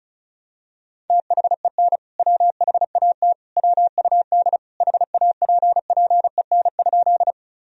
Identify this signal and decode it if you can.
Morse code, heard as a keyed tone, about 36 wpm, 700 Hz. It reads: THEN WHAT WUD HAPPEN?
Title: Sentences from Top 400 Words and Abbreviations 35wpm